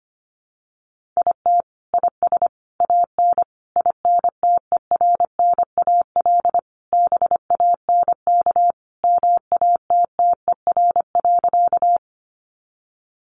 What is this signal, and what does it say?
Morse code, heard as a keyed tone, about 25 wpm, 700 Hz. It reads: IT IS AN INTERNAL BANK MATTER.
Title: Sentences from Top 1400 Words 25wpm